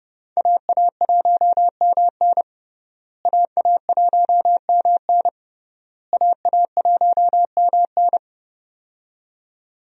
Morse code, heard as a keyed tone, about 30 wpm, 700 Hz. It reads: AA1MN AA1MN AA1MN
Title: USA Call Signs - 5 Characters - Repeated 3x - 30wpm